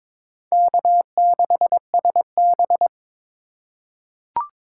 Morse code, heard as a keyed tone, about 22 wpm, 700 Hz. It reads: K6SB
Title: USA Call Signs - 4 Characters - 4x Character Spacing - 22wpm